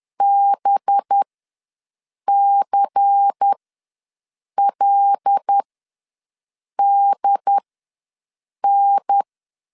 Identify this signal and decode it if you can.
Morse code, heard as a keyed tone, about 10 wpm, spaced as for 6 wpm, 800 Hz. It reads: BCLDN
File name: LEZIONI / 5 LBVC / 5 LBVC ws 10wpm 3733